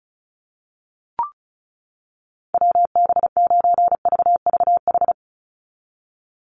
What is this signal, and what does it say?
Morse code, heard as a keyed tone, about 34 wpm, 700 Hz. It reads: WB9VVH
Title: USA Call Signs 35wpm